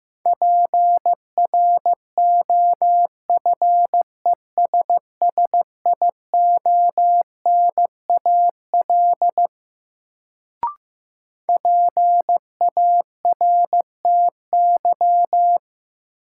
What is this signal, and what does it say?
Morse code, heard as a keyed tone, about 16 wpm, 700 Hz. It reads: PROFESSIONAL PARTY
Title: Sentences from Top 1400 Words - New Word Drill Alt - 15wpm